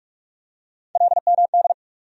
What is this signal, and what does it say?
Morse code, heard as a keyed tone, about 46 wpm, 700 Hz. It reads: LMD
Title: USA Call Sign Postfixes - 3 Letters 45wpm